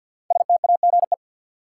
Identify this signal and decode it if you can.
Morse code, heard as a keyed tone, about 51 wpm, 700 Hz. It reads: STAGE
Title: Sentences from Top 1200 Words - New Word Drill Alt - 50wpm